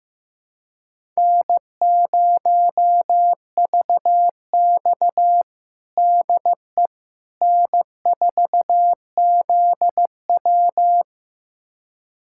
Morse code, heard as a keyed tone, about 16 wpm, 700 Hz. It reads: N0VX DE N4ZW
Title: USA Callsign DE Callsign - 4 Characters - 15wpm